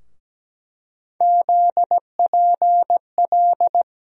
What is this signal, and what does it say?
Morse code, heard as a keyed tone, about 17 wpm, 700 Hz. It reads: ZPL
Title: USA Call Sign Postfixes - 3 Letters 17wpm